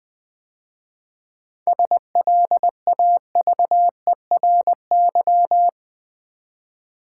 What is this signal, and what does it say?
Morse code, heard as a keyed tone, about 20 wpm, 700 Hz. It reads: SLAVERY
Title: 7 Letter Words - First 3 Letters Given - 20wpm